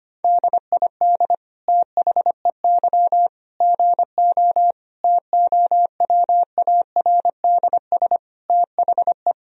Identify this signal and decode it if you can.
Morse code, heard as a keyed tone, about 25 wpm, 700 Hz. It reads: DID THEY GO TOWARDS THE
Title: Sentences from Top 1000 Words - Review Part 2 - 25wpm